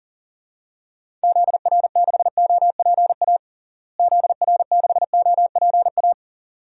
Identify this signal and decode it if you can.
Morse code, heard as a keyed tone, about 40 wpm, 700 Hz. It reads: ZR6OPA ZR6OPA
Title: DXCC-100 Call Signs - 6 Characters - Repeated 2x - 40wpm